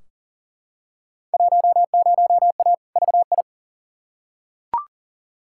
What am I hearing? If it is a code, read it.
Morse code, heard as a keyed tone, about 40 wpm, 700 Hz. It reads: 10A VI
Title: Field Day Exchanges Encore - 40wpm